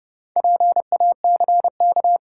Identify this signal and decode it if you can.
Morse code, heard as a keyed tone, about 29 wpm, 700 Hz. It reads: PACK
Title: Sentences from Top 1300 Words - New Word Drill Alt - 30wpm